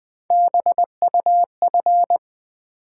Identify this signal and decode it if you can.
Morse code, heard as a keyed tone, about 20 wpm, 700 Hz. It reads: BUF
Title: Build A Word - 3 Letters At A Time - 6-Letter Words - 20wpm